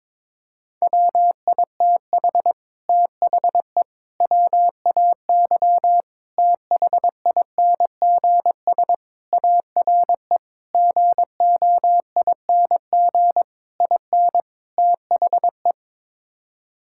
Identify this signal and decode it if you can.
Morse code, heard as a keyed tone, about 22 wpm, 700 Hz. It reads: WITH THE WAY THINGS ARE GOING IN THE
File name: practice / 22wpm / Sets of 8 Words - 22wpm